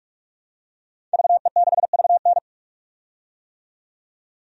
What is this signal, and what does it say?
Morse code, heard as a keyed tone, about 46 wpm, 700 Hz. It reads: VE6VN
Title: DXCC-100 Call Signs - 5 Characters - 45wpm